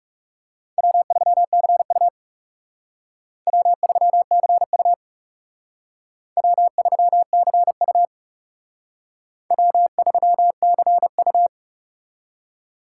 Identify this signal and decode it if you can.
Morse code, heard as a keyed tone, about 37 wpm, 700 Hz. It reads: W3CU W3CU W3CU W3CU
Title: Speed Racing - USA 1x2 Call Signs 30wpm